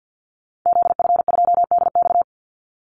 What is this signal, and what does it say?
Morse code, heard as a keyed tone, about 50 wpm, 700 Hz. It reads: ZF2DX
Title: Call Signs - 50wpm